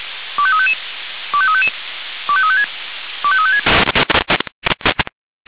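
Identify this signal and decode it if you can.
Five-tone selective call (ZVEI), heard as quick runs of five tones, tones of about 70 ms. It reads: 25355 25355 25356 25356
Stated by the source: Received by SDR, ZVEI five-tone selective call.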